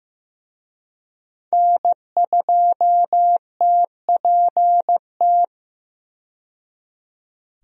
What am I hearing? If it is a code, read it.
Morse code, heard as a keyed tone, about 15 wpm, 700 Hz. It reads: N2TPT
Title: USA Call Signs - 5 Characters - 15wpm